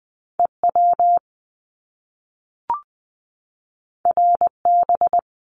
Morse code, heard as a keyed tone, about 20 wpm, 700 Hz. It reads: EW RB
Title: Top Two Letter Combinations in Words 20wpm